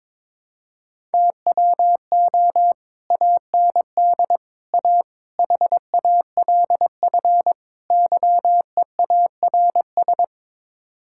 Morse code, heard as a keyed tone, about 22 wpm, 700 Hz. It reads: TWO AND A HALF YEARS
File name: practice / 22wpm / Build A Phrase - 5-Word Phrases - 22wpm